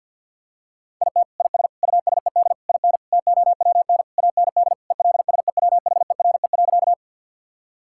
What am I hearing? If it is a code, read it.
Morse code, heard as a keyed tone, about 50 wpm, 700 Hz. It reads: IT IS USED IN TOWN AND ELSEWHERE.
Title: Sentences from Top 1600 Words 50wpm